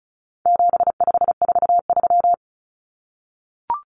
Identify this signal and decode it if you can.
Morse code, heard as a keyed tone, about 34 wpm, 700 Hz. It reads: 7543
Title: Common Cut Numbers - 4-Digit 35wpm